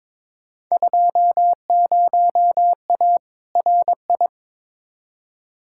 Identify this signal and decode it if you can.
Morse code, heard as a keyed tone, about 22 wpm, 700 Hz. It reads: 20A RI
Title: Field Day Exchanges Encore - 22wpm